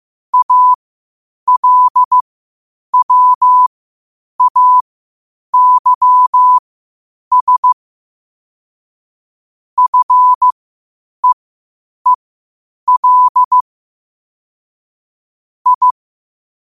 Morse code, heard as a keyed tone, about 15 wpm, spaced as for 8 wpm, 1000 Hz. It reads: ALWAYS FEEL I
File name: 8wpm